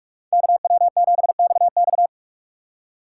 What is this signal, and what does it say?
Morse code, heard as a keyed tone, about 46 wpm, 690 Hz. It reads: KW7XX
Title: USA Call Signs - 5 Characters - 6x Character Spacing - 45wpm